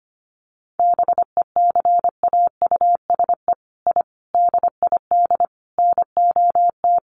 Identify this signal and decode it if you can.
Morse code, heard as a keyed tone, about 25 wpm, 700 Hz. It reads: BECAUSE I DID NOT
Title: Sets of 5 Words Encore2 25wpm